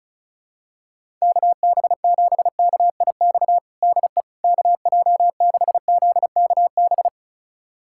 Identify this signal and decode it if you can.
Morse code, heard as a keyed tone, about 36 wpm, 700 Hz. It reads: KB7KIX DE KJ6ZKB
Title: USA Callsign DE Callsign 35wpm